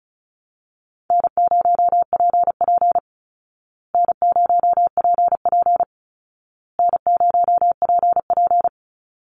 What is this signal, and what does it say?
Morse code, heard as a keyed tone, about 34 wpm, 700 Hz. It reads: N0PP N0PP N0PP
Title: Easy Call Signs Repeated 3x - 35wpm